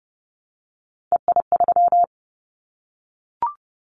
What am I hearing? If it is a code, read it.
Morse code, heard as a keyed tone, about 30 wpm, 700 Hz. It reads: EI3
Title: Callsign Prefixes for Top 50 DXCC - 30wpm